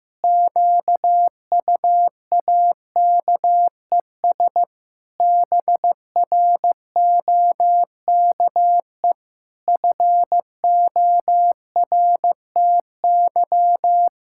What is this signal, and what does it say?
Morse code, heard as a keyed tone, about 15 wpm, 700 Hz. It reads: QUAKES BROKE FORTY
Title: Pangram Challenge v2 - 15wpm